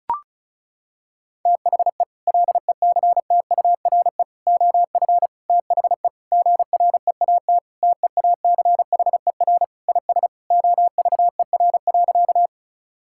Morse code, heard as a keyed tone, about 34 wpm, 700 Hz. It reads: THE LECTURE OF THE GREAT TEACHER IS OVER.
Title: Sentences from Top 1600 Words 35wpm